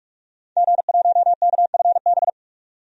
Morse code, heard as a keyed tone, about 46 wpm, 700 Hz. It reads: G1KFB
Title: DXCC-100 Call Signs - 5 Characters - 45wpm